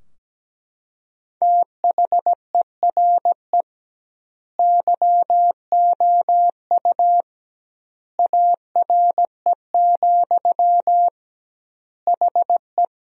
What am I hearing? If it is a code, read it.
Morse code, heard as a keyed tone, about 17 wpm, 700 Hz. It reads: THERE YOU ARE, HE
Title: 2x Word-Spacing - Sentences from Top 100 Words - 17wpm